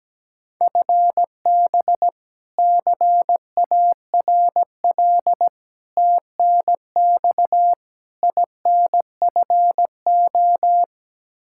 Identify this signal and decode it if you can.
Morse code, heard as a keyed tone, about 17 wpm, 700 Hz. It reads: FB CARL TNX INFO
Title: Practice QSO - Exchange 3 - More Info - 17wpm